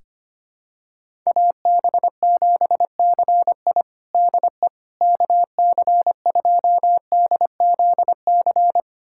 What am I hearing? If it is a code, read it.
Morse code, heard as a keyed tone, about 25 wpm, 700 Hz. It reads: AB7CI DE KC2DZC
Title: USA Callsign DE Callsign 25wpm